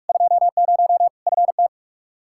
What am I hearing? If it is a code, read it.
Morse code, heard as a keyed tone, about 41 wpm, 700 Hz. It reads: 20 FT
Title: Practice QSO - Exchange 2 - Info - 45wpm